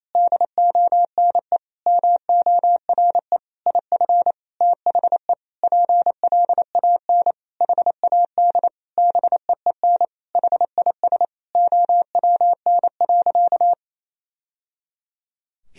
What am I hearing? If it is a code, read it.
Morse code, heard as a keyed tone, about 28 wpm, 700 Hz. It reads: DONE MORE IF THE PLAN HAD BEEN HIS OWN.
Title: Sentences from Top 1000 Words - Review Part 2 - 28wpm